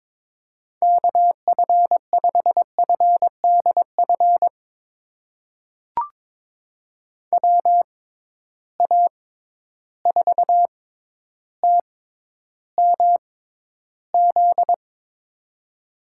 Morse code, heard as a keyed tone, about 22 wpm, spaced as for 11 wpm, 700 Hz. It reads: KF5FDF WA4TMZ
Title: USA Call Signs - 6 Characters - 6x Character Spacing - 22wpm